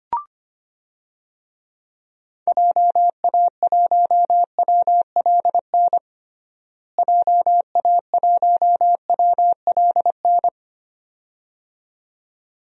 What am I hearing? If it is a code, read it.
Morse code, heard as a keyed tone, about 25 wpm, 700 Hz. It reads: JA1WLN JA1WLN
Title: DXCC-100 Call Signs - 6 Characters - Repeated 2x - 25wpm